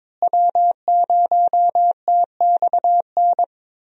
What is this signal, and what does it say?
Morse code, heard as a keyed tone, about 21 wpm, 700 Hz. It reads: W0TXN
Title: USA Call Signs - 5 Characters - Repeated 3x - 22wpm